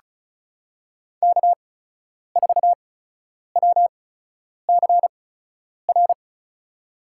Morse code, heard as a keyed tone, about 36 wpm, spaced as for 10 wpm, 700 Hz. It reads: K4WCR
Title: USA Call Signs - 8x Character Spacing - 35wpm